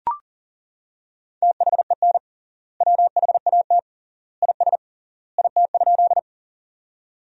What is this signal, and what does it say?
Morse code, heard as a keyed tone, about 40 wpm, 700 Hz. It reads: THEN WHAT IS IT?
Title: 3x Word-Spacing - Sentences from Top 100 Words - 40wpm